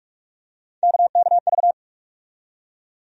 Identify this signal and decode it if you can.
Morse code, heard as a keyed tone, about 42 wpm, 700 Hz. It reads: KKV
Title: USA Call Sign Postfixes - 3 Letters 45wpm